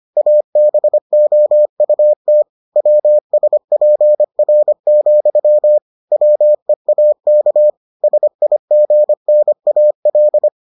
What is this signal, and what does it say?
Morse code, heard as a keyed tone, about 25 wpm, 600 Hz. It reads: ABOUT WSPR, WEAK SIGNAL